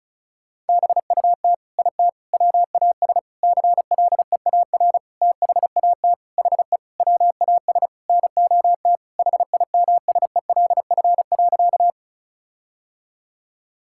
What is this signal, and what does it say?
Morse code, heard as a keyed tone, about 36 wpm, 700 Hz. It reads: BUT IT WAS CLEAR THAT HE WAS NOT HIMSELF.
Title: Sentences from Top 1600 Words 35wpm